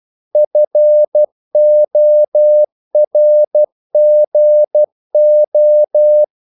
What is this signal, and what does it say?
Morse code, heard as a keyed tone, about 12 wpm, 600 Hz. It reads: FORGO
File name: CW / Practice / EnglishQuotations0112